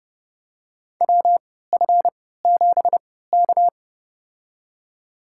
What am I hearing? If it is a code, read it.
Morse code, heard as a keyed tone, about 30 wpm, spaced as for 17 wpm, 700 Hz. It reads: WF7K
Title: USA Call Signs - 3x Character Spacing - 30wpm